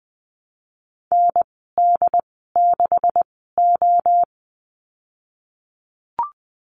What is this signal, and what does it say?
Morse code, heard as a keyed tone, about 20 wpm, spaced as for 15 wpm, 700 Hz. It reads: ND6O
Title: 154 Lesson 25g - Letter p - Call Signs 20wpm